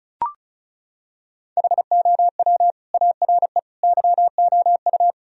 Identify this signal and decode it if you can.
Morse code, heard as a keyed tone, about 34 wpm, 700 Hz. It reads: HOW ARE YOU
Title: Sentences from Top 1000 Words - Review Part 4 - 35wpm